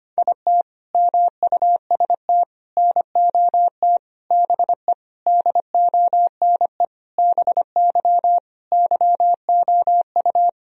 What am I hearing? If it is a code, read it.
Morse code, heard as a keyed tone, about 25 wpm, 700 Hz. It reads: IT MUST NOT BE DONE BY YOU
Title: Sentences from Top 1000 Words - Review Part 1 - 25wpm